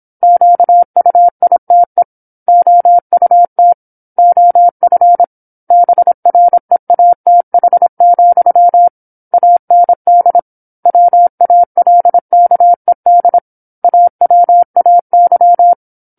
Morse code, heard as a keyed tone, about 27 wpm, 700 Hz. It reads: QUITE OUT OF BREATH, AND WALKED AWAY